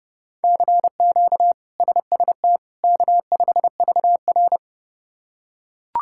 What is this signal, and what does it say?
Morse code, heard as a keyed tone, about 31 wpm, 700 Hz. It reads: CQ SST K5VR
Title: K1USN SST QSO Practice - 30wpm